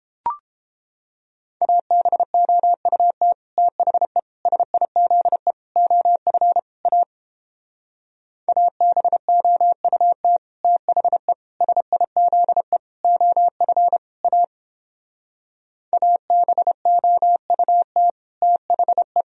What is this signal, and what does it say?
Morse code, heard as a keyed tone, about 29 wpm, 700 Hz. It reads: ABOUT THE SIZE OF A ABOUT THE SIZE OF A ABOUT THE